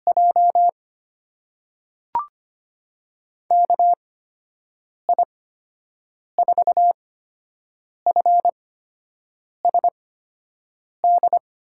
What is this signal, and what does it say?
Morse code, heard as a keyed tone, about 25 wpm, spaced as for 7 wpm, 700 Hz. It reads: J KI4FSD
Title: USA Call Signs - 8x Character Spacing - 25wpm